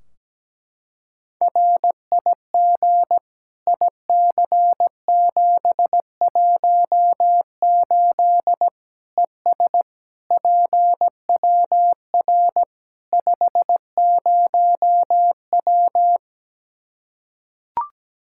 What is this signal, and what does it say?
Morse code, heard as a keyed tone, about 17 wpm, 700 Hz. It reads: RIG IC718 ES PWR 50W
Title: Practice QSO - Exchange 2 - Info - 17wpm